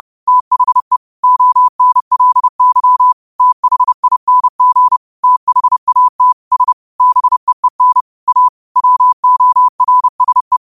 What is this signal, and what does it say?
Morse code, heard as a keyed tone, about 30 wpm, 1000 Hz. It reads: THE ONLY THING THAT S BEEN A WORSE
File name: qotdfiles / 30wpm